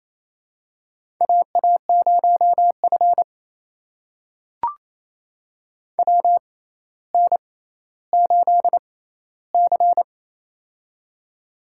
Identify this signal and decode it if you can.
Morse code, heard as a keyed tone, about 28 wpm, spaced as for 14 wpm, 700 Hz. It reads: AA0F WN8C
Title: USA Call Signs - 4 Characters - 6x Character Spacing - 28wpm